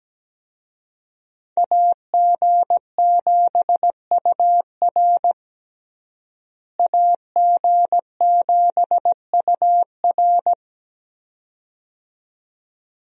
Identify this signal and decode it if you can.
Morse code, heard as a keyed tone, about 17 wpm, 700 Hz. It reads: AG7UR AG7UR
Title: USA Call Signs - 5 Characters - Repeated 2x - 17wpm